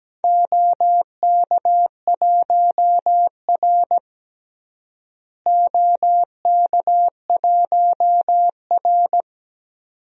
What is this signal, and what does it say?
Morse code, heard as a keyed tone, about 17 wpm, 700 Hz. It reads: OK1R OK1R
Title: DXCC-100 Call Signs - 4 Characters - Repeated 2x - 17wpm